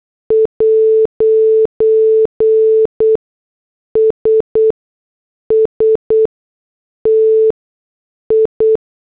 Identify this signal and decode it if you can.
Morse code, heard as a keyed tone, about 8 wpm, spaced as for 6 wpm, 440 Hz. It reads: 'S STI
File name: morse / headlines